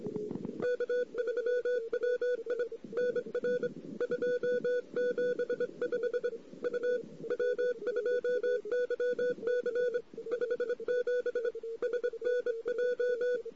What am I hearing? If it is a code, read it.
Morse code, heard as a keyed tone, about 24 wpm, 500 Hz. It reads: K3WI NR 275 U W2YC 57 SNJ
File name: CW / 2008-ss-cw-k5tr-op-n5rz -14:00